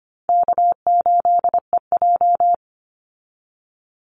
Morse code, heard as a keyed tone, about 25 wpm, 700 Hz. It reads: K8EJ
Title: Easy Call Signs Repeated 2x - 25wpm